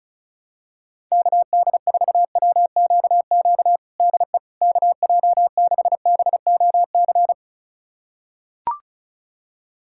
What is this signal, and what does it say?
Morse code, heard as a keyed tone, about 34 wpm, 690 Hz. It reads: KD4WQQ DE KJ6BOC
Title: USA Callsign DE Callsign - 6 Characters - 35wpm